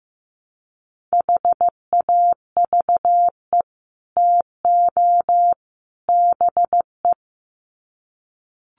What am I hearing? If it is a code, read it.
Morse code, heard as a keyed tone, about 15 wpm, 700 Hz. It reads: HAVE TO BE
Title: Sets of 3 Words 15wpm